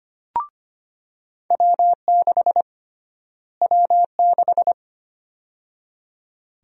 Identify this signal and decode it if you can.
Morse code, heard as a keyed tone, about 25 wpm, 700 Hz. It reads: W6 W6
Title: USA Call Sign Prefixes - Repeated 2x - 25wpm